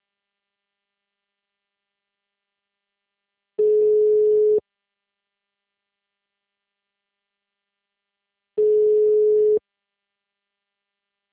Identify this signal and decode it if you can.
Ringback tone (Europe, 425 Hz).